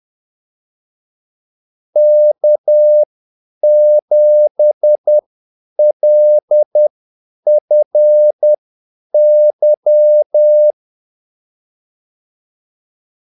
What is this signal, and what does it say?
Morse code, heard as a keyed tone, about 10 wpm, spaced as for 8 wpm, 600 Hz. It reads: K7LFY
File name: Callsigns-10-8wpm